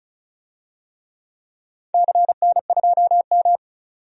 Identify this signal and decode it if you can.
Morse code, heard as a keyed tone, about 36 wpm, 700 Hz. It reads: CN2M
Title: DXCC-100 Call Signs - 4 Characters - 35wpm